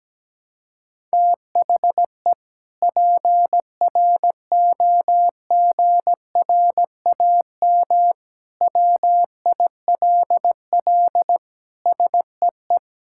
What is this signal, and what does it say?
Morse code, heard as a keyed tone, about 17 wpm, 700 Hz. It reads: THE PROGRAM WILL SEE
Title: Sentences from Top 1000 Words - Review Part 5 - 17wpm